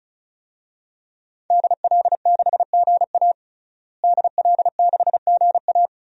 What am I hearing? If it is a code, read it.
Morse code, heard as a keyed tone, about 36 wpm, 700 Hz. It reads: DL6GA DL6GA
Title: DXCC-100 Call Signs - 5 Characters - Repeated 2x - 35wpm